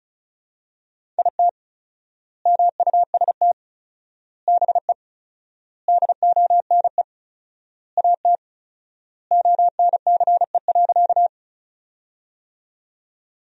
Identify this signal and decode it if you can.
Morse code, heard as a keyed tone, about 36 wpm, 700 Hz. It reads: IT MUST BE DONE AT ONCE.
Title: 4x Word-Spacing - Sentences from Top 500 Words - 35wpm